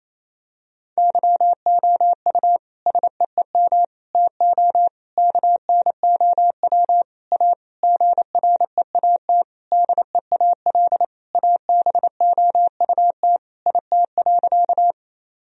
Morse code, heard as a keyed tone, about 28 wpm, 700 Hz. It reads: YOU SEEM TO KNOW A GREAT DEAL ABOUT IT.
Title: Sentences from Top 1000 Words - Review Part 1 - 28wpm